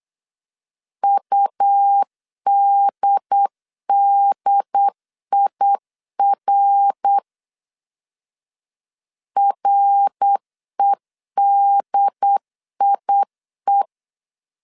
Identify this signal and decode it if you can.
Morse code, heard as a keyed tone, about 8 wpm, 800 Hz. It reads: UDDIR REDIE
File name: CW / DET / 1 3 1500C S B 8wpm 4141